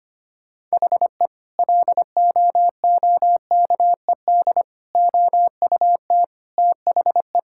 Morse code, heard as a keyed tone, about 25 wpm, 700 Hz. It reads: HE LOOKED OUT THE
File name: practice / 25wpm / Sentences with 2-Word Phrases - 25wpm